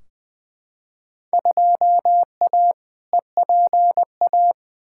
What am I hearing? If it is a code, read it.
Morse code, heard as a keyed tone, about 20 wpm, 700 Hz. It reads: 2A EPA
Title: Field Day Exchanges - 20wpm